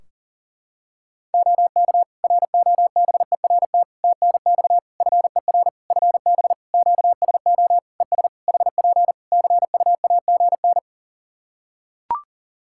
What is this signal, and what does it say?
Morse code, heard as a keyed tone, about 40 wpm, 700 Hz. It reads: OK ROBERT TNX FER FB QSO ES HP CUAGN
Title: K1USN SST QSO Practice 40wpm